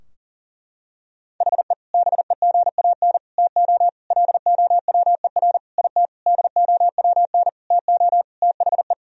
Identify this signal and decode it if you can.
Morse code, heard as a keyed tone, about 40 wpm, 700 Hz. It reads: HE BEGAN TO LOWER IT DOWN TO THE
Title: Sentences from Top 1000 Words - Review Part 4 - 40wpm